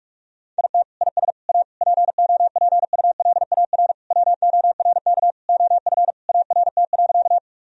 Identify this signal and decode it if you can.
Morse code, heard as a keyed tone, about 41 wpm, 700 Hz. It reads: IT IS A POPULAR WORK OF ART.
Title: Sentences from Top 1000 Words - Review Part 3 - 45wpm